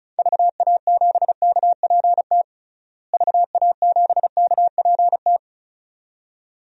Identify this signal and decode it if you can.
Morse code, heard as a keyed tone, about 34 wpm, 700 Hz. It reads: VA7KPT VA7KPT
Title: DXCC-100 Call Signs - 6 Characters - Repeated 2x - 35wpm